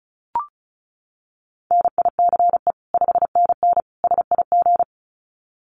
Morse code, heard as a keyed tone, about 34 wpm, 700 Hz. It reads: NICE 5NN SIG
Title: Common QSO Phrases 35wpm